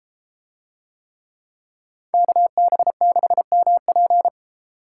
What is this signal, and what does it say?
Morse code, heard as a keyed tone, about 32 wpm, 700 Hz. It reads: KB6MP